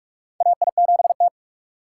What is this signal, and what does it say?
Morse code, heard as a keyed tone, about 44 wpm, 700 Hz. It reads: AI7T
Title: USA Call Signs - 4 Characters - 3x Character Spacing - 45wpm